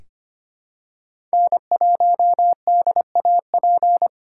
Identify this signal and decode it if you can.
Morse code, heard as a keyed tone, about 25 wpm, 700 Hz. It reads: N1DAP
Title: USA Call Signs - 5 Characters - 4x Character Spacing - 25wpm